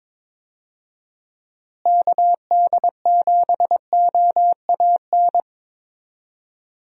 Morse code, heard as a keyed tone, about 21 wpm, 700 Hz. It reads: KD7OAN
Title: USA Call Signs - 6 Characters - Repeated 3x - 22wpm